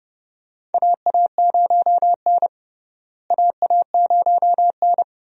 Morse code, heard as a keyed tone, about 32 wpm, 700 Hz. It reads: AA0N AA0N
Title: USA Call Signs - 4 Characters - Repeated 2x - 30wpm